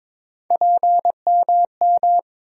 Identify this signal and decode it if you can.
Morse code, heard as a keyed tone, about 22 wpm, 700 Hz. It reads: PMM
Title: USA Call Sign Postfixes - 3 Letters 22wpm